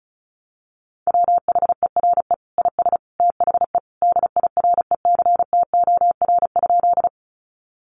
Morse code, heard as a keyed tone, about 34 wpm, 700 Hz. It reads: WHERE IS THE DIRECTOR?
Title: Sentences from Top 1300 Words 35wpm